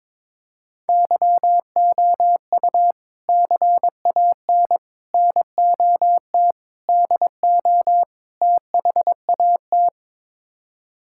Morse code, heard as a keyed tone, about 22 wpm, 700 Hz. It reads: YOU CAN NOT DO THAT
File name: practice / 22wpm / Build A Phrase - 5-Word Phrases - 22wpm